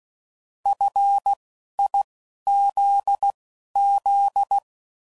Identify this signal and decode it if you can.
Morse code, heard as a keyed tone, about 16 wpm, spaced as for 11 wpm, 800 Hz. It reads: FIZZ